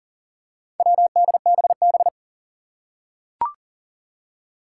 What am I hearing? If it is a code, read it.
Morse code, heard as a keyed tone, about 40 wpm, 700 Hz. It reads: WDBB
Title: Four Challenging Letter-Number Combinations 40wpm